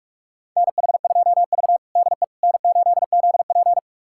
Morse code, heard as a keyed tone, about 46 wpm, 700 Hz. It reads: NH2V DE N8ZP
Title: USA Callsign DE Callsign - 4 Characters - 45wpm